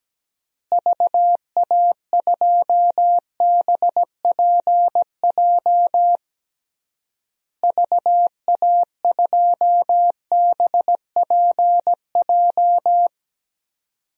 Morse code, heard as a keyed tone, about 17 wpm, 700 Hz. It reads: VA2BPJ VA2BPJ
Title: DXCC-100 Call Signs - 6 Characters - Repeated 3x - 17wpm